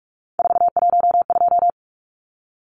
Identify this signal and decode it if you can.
Morse code, heard as a keyed tone, about 44 wpm, 700 Hz. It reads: 412
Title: Numbers - 3-Digit 45wpm